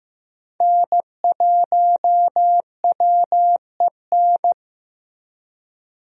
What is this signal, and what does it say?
Morse code, heard as a keyed tone, about 15 wpm, 700 Hz. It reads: N1WEN
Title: USA Call Signs - 5 Characters - Repeated 2x - 15wpm